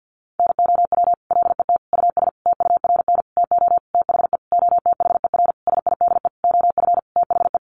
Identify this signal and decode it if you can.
Morse code, heard as a keyed tone, about 50 wpm, 700 Hz. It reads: NOW LET US TURN TO THE OTHER SIDE OF THE
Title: Sentences from Top 1400 Words 50wpm